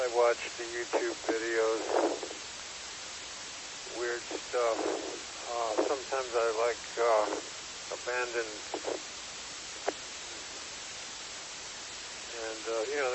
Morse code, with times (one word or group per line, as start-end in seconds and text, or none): none